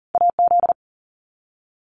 Morse code, heard as a keyed tone, about 40 wpm, 700 Hz. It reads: AZ
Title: US States Repeated 3x 40wpm